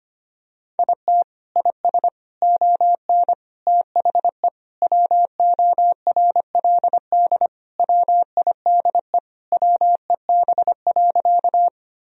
Morse code, heard as a keyed tone, about 25 wpm, 700 Hz. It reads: IT IS ON THE WORLD WIDE WEB.
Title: Sentences from Top 1000 Words - Review Part 4 - 25wpm